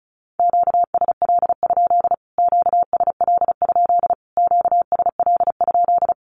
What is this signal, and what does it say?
Morse code, heard as a keyed tone, about 34 wpm, 700 Hz. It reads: QSL? QSL? QSL?
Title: Q-Codes Repeated 3x 35wpm